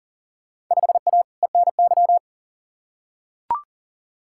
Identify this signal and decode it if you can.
Morse code, heard as a keyed tone, about 40 wpm, 700 Hz. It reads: 5A ENY
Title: Field Day Exchanges Encore - 40wpm